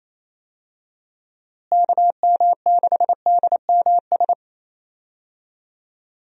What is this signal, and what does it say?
Morse code, heard as a keyed tone, about 28 wpm, 700 Hz. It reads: KM6DMS
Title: USA Call Signs - 6 Characters - Repeated 3x - 28wpm